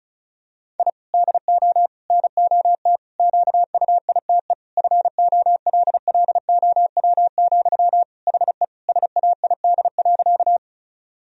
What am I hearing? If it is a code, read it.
Morse code, heard as a keyed tone, about 34 wpm, 700 Hz. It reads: I DO NOT QUITE FOLLOW, HE SAID.
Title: Sentences from Top 1000 Words - Review Part 1 - 35wpm